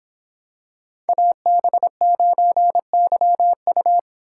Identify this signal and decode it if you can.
Morse code, heard as a keyed tone, about 26 wpm, 700 Hz. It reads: AB9YU